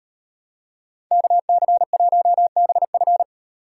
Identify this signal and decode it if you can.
Morse code, heard as a keyed tone, about 38 wpm, 700 Hz. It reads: KC1BF